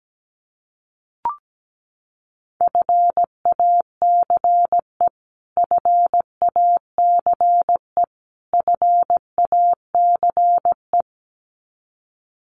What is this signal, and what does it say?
Morse code, heard as a keyed tone, about 17 wpm, 700 Hz. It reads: FACE FACE FACE
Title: Top 500 Words Repeated 17wpm